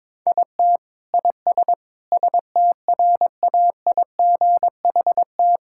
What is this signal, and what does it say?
Morse code, heard as a keyed tone, about 22 wpm, 700 Hz. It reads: IT IS STRAIGHT
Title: Sentences from Top 1000 Words - Review Part 6 - 22wpm